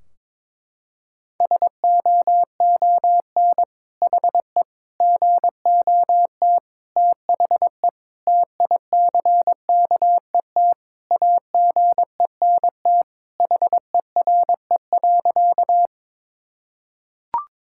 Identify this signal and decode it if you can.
Morse code, heard as a keyed tone, about 22 wpm, 700 Hz. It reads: SOON HE GOT THE TICKET AGENT HERE.
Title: Sentences from Top 1600 Words 22wpm